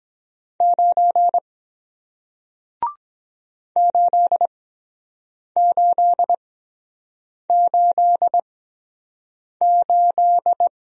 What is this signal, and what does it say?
Morse code, heard as a keyed tone, about 23 wpm, spaced as for 7 wpm, 700 Hz. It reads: 9 8888